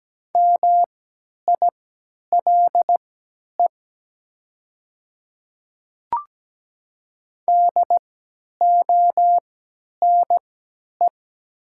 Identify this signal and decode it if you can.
Morse code, heard as a keyed tone, about 17 wpm, spaced as for 10 wpm, 690 Hz. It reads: MILE DONE
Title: 3x Character Spacing - Top 500 Words - 17wpm